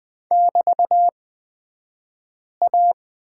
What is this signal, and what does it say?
Morse code, heard as a keyed tone, about 20 wpm, spaced as for 5 wpm, 700 Hz. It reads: <BT>A
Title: 223 Lesson 36d - BT - Two Characters 20wpm